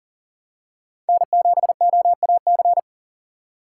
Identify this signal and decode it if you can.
Morse code, heard as a keyed tone, about 40 wpm, 700 Hz. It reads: N7OAC
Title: USA Call Signs - 6x Character Spacing - 40wpm